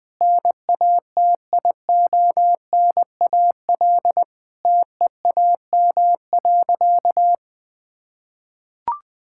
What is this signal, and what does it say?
Morse code, heard as a keyed tone, about 20 wpm, 700 Hz. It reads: NATIONAL TEAM.
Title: Sentences with 3-Word Phrases - 20wpm